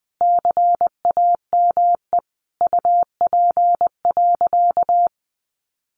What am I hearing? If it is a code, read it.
Morse code, heard as a keyed tone, about 20 wpm, 700 Hz. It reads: CAME UP.
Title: Sentences from Top 1500 Words 20wpm